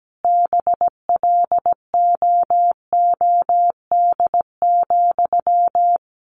Morse code, heard as a keyed tone, about 17 wpm, 700 Hz. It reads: BLOOD,
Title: Sentences from Top 1100 Words 17wpm